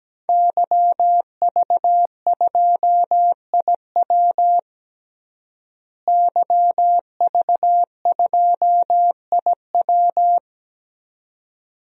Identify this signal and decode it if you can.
Morse code, heard as a keyed tone, about 17 wpm, 700 Hz. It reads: YV2IW YV2IW
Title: DXCC-100 Call Signs - 5 Characters - Repeated 2x - 17wpm